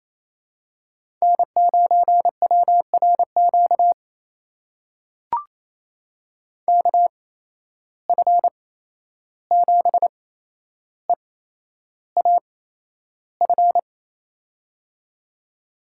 Morse code, heard as a keyed tone, about 28 wpm, spaced as for 11 wpm, 700 Hz. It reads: N9WRQ KF7EAF
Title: USA Call Signs - 8x Character Spacing - 28wpm